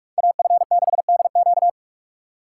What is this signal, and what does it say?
Morse code, heard as a keyed tone, about 38 wpm, 700 Hz. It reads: AF6DQ